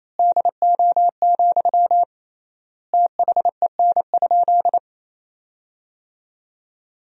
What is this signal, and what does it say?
Morse code, heard as a keyed tone, about 28 wpm, 700 Hz. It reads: DO, THEN?
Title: 3x Word-Spacing - Sentences from Top 100 Words - 28wpm